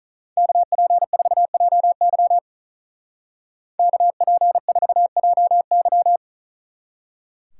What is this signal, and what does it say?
Morse code, heard as a keyed tone, about 43 wpm, 700 Hz. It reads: KP4JY KP4JY
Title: Speed Racing - USA Call Signs 35wpm